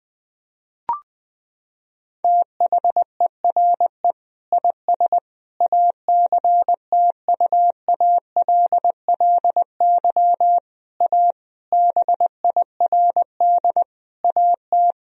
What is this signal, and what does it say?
Morse code, heard as a keyed tone, about 21 wpm, 700 Hz. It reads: THERE IS ACTUALLY A BIRD AT
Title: Sentences from Top 1000 Words - Review Part 4 - 20wpm